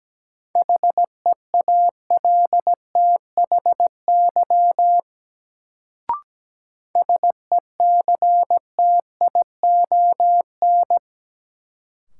Morse code, heard as a keyed tone, about 17 wpm, 700 Hz. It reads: HEALTHY SECTION
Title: Sentences from Top 1000 Words - Review Part 5 - New Word Drill - 17wpm